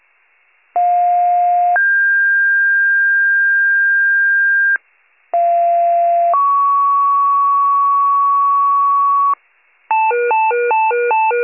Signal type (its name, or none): two-tone page